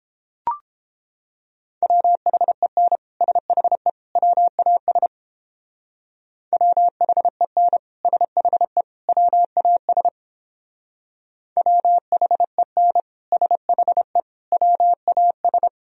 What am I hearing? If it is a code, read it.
Morse code, heard as a keyed tone, about 29 wpm, 700 Hz. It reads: WHEN SHE WAS WHEN SHE WAS WHEN SHE WAS